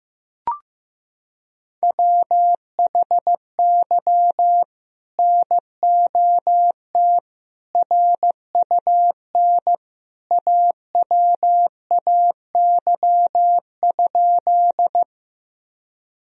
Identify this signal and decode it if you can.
Morse code, heard as a keyed tone, about 15 wpm, 700 Hz. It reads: WHY NOT RUN AWAY?
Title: Sentences from Top 1000 Words - Review Part 1 - 15wpm